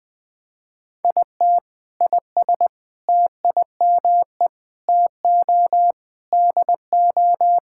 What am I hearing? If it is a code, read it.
Morse code, heard as a keyed tone, about 21 wpm, 700 Hz. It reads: IT IS TIME TO DO
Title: Sentences with 5-Word Phrases - 20wpm